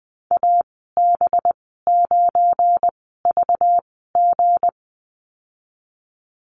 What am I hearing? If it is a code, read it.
Morse code, heard as a keyed tone, about 20 wpm, spaced as for 14 wpm, 700 Hz. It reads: AB9VG
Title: 245 Lesson 39g - Letter z - Call Signs 20wpm